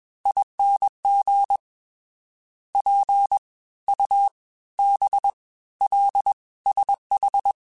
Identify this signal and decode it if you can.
Morse code, heard as a keyed tone, about 21 wpm, spaced as for 15 wpm, 800 Hz. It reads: ING PUBLSH